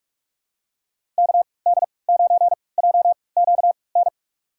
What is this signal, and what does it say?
Morse code, heard as a keyed tone, about 42 wpm, spaced as for 25 wpm, 700 Hz. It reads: KD9JQN
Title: USA Call Signs - 6 Characters - 3x Character Spacing - 45wpm